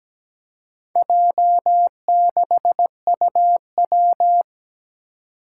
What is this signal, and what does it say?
Morse code, heard as a keyed tone, about 17 wpm, 700 Hz. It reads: J6UW
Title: Four Challenging Letter-Number Combinations 17wpm